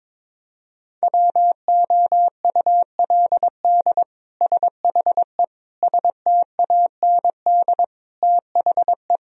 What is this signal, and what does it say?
Morse code, heard as a keyed tone, about 22 wpm, 700 Hz. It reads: WOULD SHE STAND THE
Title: Sentences from Top 1000 Words - Review Part 3 - 22wpm